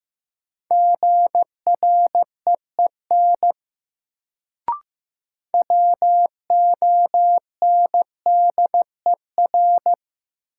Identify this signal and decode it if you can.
Morse code, heard as a keyed tone, about 16 wpm, 700 Hz. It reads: GREEN WONDER
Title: Sentences from Top 500 Words - New Word Drill - 15wpm